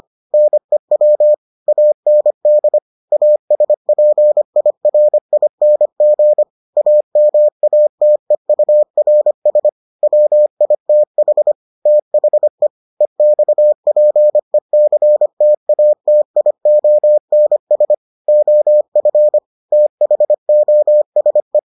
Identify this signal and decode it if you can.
Morse code, heard as a keyed tone, about 25 wpm, 600 Hz. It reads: NEW AND ASPIRING AMATEURS WITH THE EXPECTATIONS OF THOSE